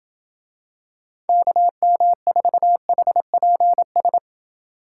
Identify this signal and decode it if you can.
Morse code, heard as a keyed tone, about 27 wpm, 700 Hz. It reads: KM4HPS